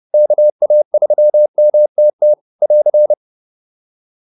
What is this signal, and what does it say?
Morse code, heard as a keyed tone, about 30 wpm, 590 Hz. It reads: KA3MTT <AR>